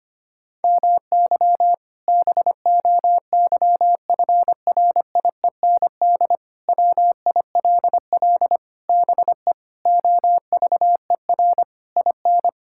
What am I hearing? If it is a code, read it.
Morse code, heard as a keyed tone, about 25 wpm, 700 Hz. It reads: MY BOYFRIEND WILL BE OVER IN
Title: Sentences from Top 1000 Words - Review Part 5 - 25wpm